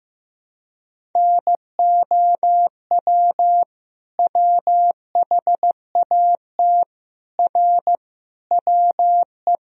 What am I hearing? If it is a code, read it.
Morse code, heard as a keyed tone, about 15 wpm, 700 Hz. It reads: NOW WHAT R WE
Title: Sentences from Top 100 Words and Abbreviations 15wpm